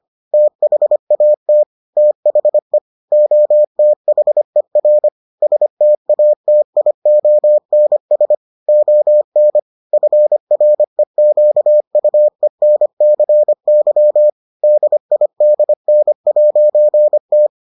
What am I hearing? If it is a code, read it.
Morse code, heard as a keyed tone, about 25 wpm, 600 Hz. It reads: THAT THE OTHER STATIONS ON FREQUENCY DIDN'T